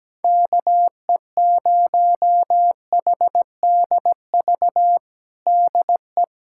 Morse code, heard as a keyed tone, about 17 wpm, 700 Hz. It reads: KE0HDV DE
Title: USA Callsign DE Callsign 17wpm